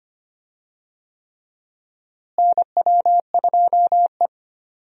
Morse code, heard as a keyed tone, about 25 wpm, 700 Hz. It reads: NW2E
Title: USA Call Signs - 4 Characters - Repeated 3x - 25wpm